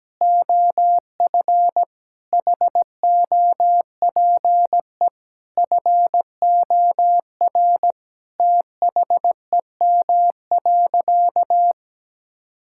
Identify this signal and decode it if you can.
Morse code, heard as a keyed tone, about 17 wpm, 700 Hz. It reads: OF HOPE FOR THEM.
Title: Sentences from Top 1000 Words - Review Part 5 - 17wpm